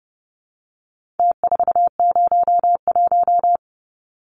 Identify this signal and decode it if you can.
Morse code, heard as a keyed tone, about 30 wpm, 700 Hz. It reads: T401
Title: Numbers - Contest Style - 4-Digit 30wpm